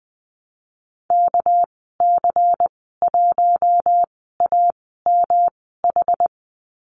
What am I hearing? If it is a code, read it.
Morse code, heard as a keyed tone, about 20 wpm, spaced as for 15 wpm, 700 Hz. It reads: KC1AMH
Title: 231 Lesson 37g - Letter x - Call Signs 20wpm